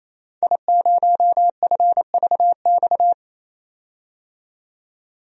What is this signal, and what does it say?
Morse code, heard as a keyed tone, about 28 wpm, 700 Hz. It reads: I0FVX
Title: DXCC-100 Call Signs - 5 Characters - 28wpm